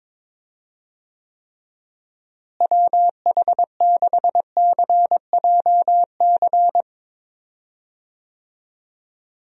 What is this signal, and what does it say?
Morse code, heard as a keyed tone, about 21 wpm, 700 Hz. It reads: WH6CJC
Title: USA Call Signs - 6 Characters - Repeated 3x - 22wpm